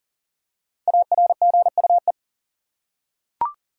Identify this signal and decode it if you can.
Morse code, heard as a keyed tone, about 40 wpm, 700 Hz. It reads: ARGUE
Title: Five Letter Words - 40wpm